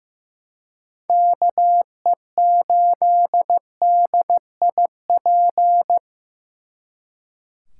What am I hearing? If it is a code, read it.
Morse code, heard as a keyed tone, about 18 wpm, 700 Hz. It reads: KE8DIP
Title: Speed Racing - USA 2x3 Call Signs 15wpm